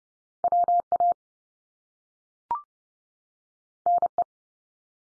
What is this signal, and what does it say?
Morse code, heard as a keyed tone, about 30 wpm, 700 Hz. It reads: WA NE
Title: US States 30WPM